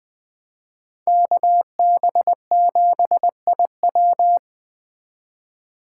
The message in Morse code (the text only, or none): KB7IW